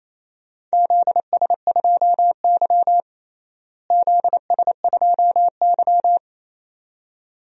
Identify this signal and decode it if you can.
Morse code, heard as a keyed tone, about 28 wpm, 700 Hz. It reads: ZS2Y ZS2Y
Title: DXCC-100 Call Signs - 4 Characters - Repeated 2x - 28wpm